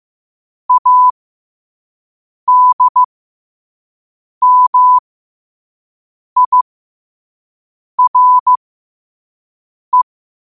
Morse code, heard as a keyed tone, about 15 wpm, spaced as for 5 wpm, 1000 Hz. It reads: ADMIRE
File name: qotdfiles / 5wpm